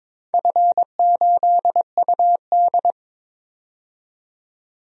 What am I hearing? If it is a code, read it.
Morse code, heard as a keyed tone, about 22 wpm, 700 Hz. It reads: F8UD
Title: DXCC-100 Call Signs - 4 Characters - 22wpm